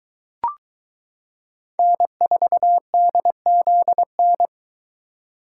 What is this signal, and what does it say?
Morse code, heard as a keyed tone, about 20 wpm, 700 Hz. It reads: N4DZN